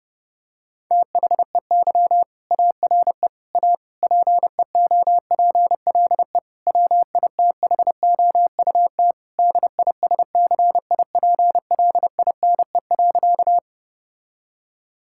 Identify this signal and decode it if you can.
Morse code, heard as a keyed tone, about 30 wpm, 700 Hz. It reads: THEY ARE A PEOPLE WITHOUT DISCIPLINE.
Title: Sentences from Top 1000 Words - Review Part 6 - 30wpm